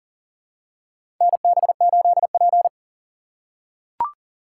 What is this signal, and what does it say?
Morse code, heard as a keyed tone, about 38 wpm, 700 Hz. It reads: NB8P
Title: USA Call Signs - 4 Characters - 3x Character Spacing - 40wpm